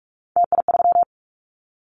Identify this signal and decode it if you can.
Morse code, heard as a keyed tone, about 46 wpm, 700 Hz. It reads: TI3
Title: Callsign Prefixes for Top 100 DXCC - 45wpm